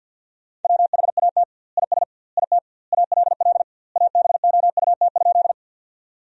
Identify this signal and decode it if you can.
Morse code, heard as a keyed tone, about 50 wpm, 700 Hz. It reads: WHAT IS IT ALL ABOUT?
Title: 2x Word-Spacing - Sentences from Top 100 Words - 50wpm